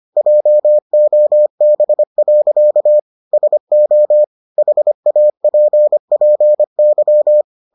Morse code, heard as a keyed tone, about 25 wpm, 600 Hz. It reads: JOB. SO HAPPY